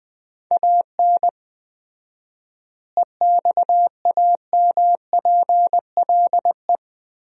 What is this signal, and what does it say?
Morse code, heard as a keyed tone, about 21 wpm, 700 Hz. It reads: AN EXAMPLE
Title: 4x Word-Spacing - Sentences from Top 400 Words - 20wpm